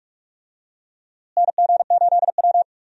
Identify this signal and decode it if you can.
Morse code, heard as a keyed tone, about 42 wpm, 700 Hz. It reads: NG8W
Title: USA Call Signs - 4 Characters - Repeated 2x - 45wpm